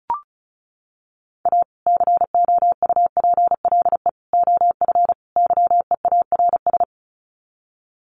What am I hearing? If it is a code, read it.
Morse code, heard as a keyed tone, about 34 wpm, 700 Hz. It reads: A COUPLE OF YEARS
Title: Sets of 4 Words 35wpm